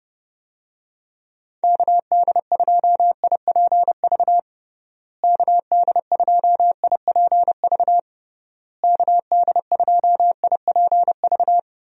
Morse code, heard as a keyed tone, about 32 wpm, 700 Hz. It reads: KD2IPV KD2IPV KD2IPV
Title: USA Call Signs - 6 Characters - Repeated 3x - 30wpm